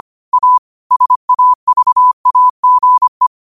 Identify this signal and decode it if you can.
Morse code, heard as a keyed tone, about 25 wpm, 1000 Hz. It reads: A SAVAGE